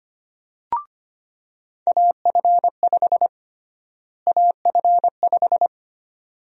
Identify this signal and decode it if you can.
Morse code, heard as a keyed tone, about 25 wpm, 700 Hz. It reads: AF5 AF5
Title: USA Call Sign Prefixes - Repeated 2x - 25wpm